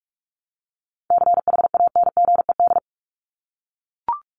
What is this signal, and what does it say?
Morse code, heard as a keyed tone, about 44 wpm, 700 Hz. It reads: CHANGED
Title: ED Suffix - Top 100 Words with Base Words 45wpm